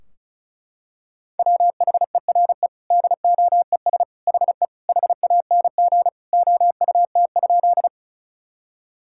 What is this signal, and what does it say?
Morse code, heard as a keyed tone, about 34 wpm, 700 Hz. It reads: WHERE DOES HE HANG OUT?
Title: Sentences from Top 1600 Words 35wpm